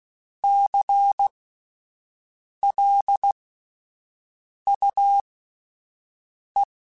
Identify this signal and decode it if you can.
Morse code, heard as a keyed tone, about 16 wpm, spaced as for 5 wpm, 800 Hz. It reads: CLUE
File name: Alphabetical / Alphabetical mp3s / Cs @5